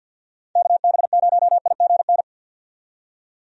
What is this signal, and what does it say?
Morse code, heard as a keyed tone, about 52 wpm, 700 Hz. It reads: KB0IGN
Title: USA Call Signs - 6 Characters - 50wpm